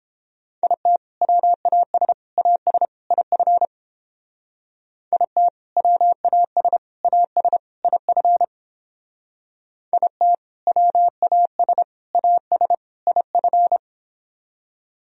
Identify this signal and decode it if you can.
Morse code, heard as a keyed tone, about 29 wpm, 700 Hz. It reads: IT WAS AS IF IT WAS AS IF IT WAS AS IF